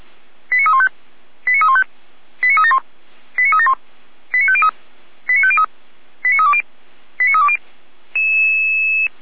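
Five-tone selective call (ZVEI), heard as quick runs of five tones, tones of about 70 ms. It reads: 89315 89315 89371 89371 89503 89503 89320 89320
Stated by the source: ZVEI five-tone call 89319 received by SDR, from the server of Feuerwehr Orlamünde (fire brigade).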